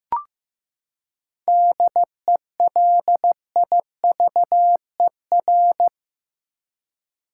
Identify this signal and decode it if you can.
Morse code, heard as a keyed tone, about 15 wpm, 700 Hz. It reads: DELIVER
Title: Seven Letter Words - 15wpm